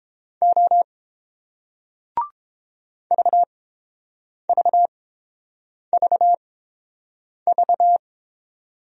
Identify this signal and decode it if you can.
Morse code, heard as a keyed tone, about 30 wpm, spaced as for 7 wpm, 700 Hz. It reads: O VVVV